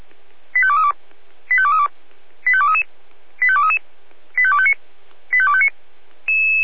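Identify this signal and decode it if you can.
Five-tone selective call (ZVEI), heard as quick runs of five tones, tones of about 70 ms. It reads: 85232 85232 85233 85233 85259 85259
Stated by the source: Five-tone ZVEI selective call received by SDR.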